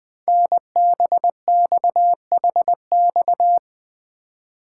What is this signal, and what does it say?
Morse code, heard as a keyed tone, about 20 wpm, 700 Hz. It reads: NBXHX
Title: Code Groups - 5 Letters - 20wpm